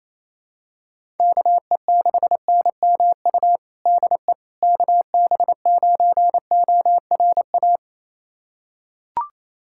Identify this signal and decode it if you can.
Morse code, heard as a keyed tone, about 28 wpm, 700 Hz. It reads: KE6NMU DE KB9ORA
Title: USA Callsign DE Callsign 28wpm